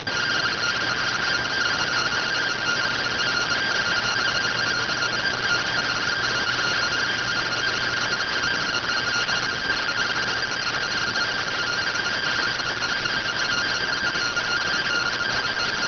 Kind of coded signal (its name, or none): two-tone teleprinter signal